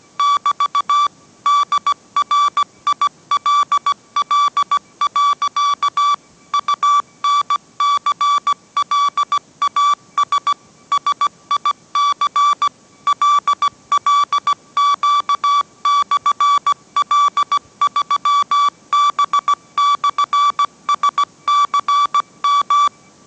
Morse code, heard as a keyed tone, about 17 wpm, 1150 Hz. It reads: <BT> DRILL. UNCLAS SIC LLQ/L3B/SCM